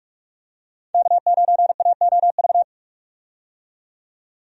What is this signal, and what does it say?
Morse code, heard as a keyed tone, about 41 wpm, 700 Hz. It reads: K9AOV
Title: USA Call Signs - 5 Characters - 45wpm